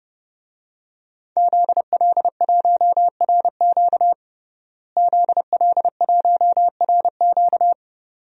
Morse code, heard as a keyed tone, about 32 wpm, 700 Hz. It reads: ZL1RQ ZL1RQ
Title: DXCC-100 Call Signs - 5 Characters - Repeated 2x - 30wpm